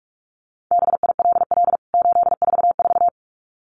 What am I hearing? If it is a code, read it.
Morse code, heard as a keyed tone, about 44 wpm, 700 Hz. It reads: BILL 844
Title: CWT Exchange 45wpm